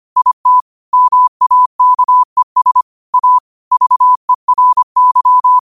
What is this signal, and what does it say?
Morse code, heard as a keyed tone, about 25 wpm, 1000 Hz. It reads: IT MAKES A VERY
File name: qotdfiles / 25wpm